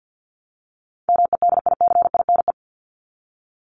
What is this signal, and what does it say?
Morse code, heard as a keyed tone, about 50 wpm, 700 Hz. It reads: MEDICINE
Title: Top 751-1000 Words 50wpm